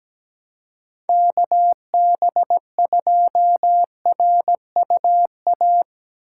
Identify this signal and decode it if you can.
Morse code, heard as a keyed tone, about 17 wpm, 700 Hz. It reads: KB2RUA
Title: USA Call Signs - 2x Character Spacing - 17wpm